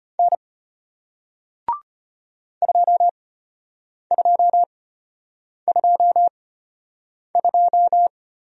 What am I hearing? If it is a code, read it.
Morse code, heard as a keyed tone, about 33 wpm, spaced as for 8 wpm, 700 Hz. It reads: N 2222